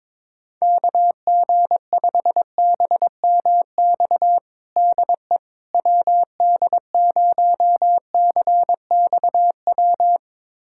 Morse code, heard as a keyed tone, about 22 wpm, 700 Hz. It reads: KG5BMX DE WD0CXW
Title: USA Callsign DE Callsign - 6 Characters - 22wpm